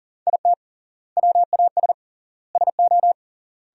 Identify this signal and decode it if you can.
Morse code, heard as a keyed tone, about 38 wpm, 700 Hz. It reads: IT WAS SO
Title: 3x Word-Spacing - Sentences from Top 500 Words - 40wpm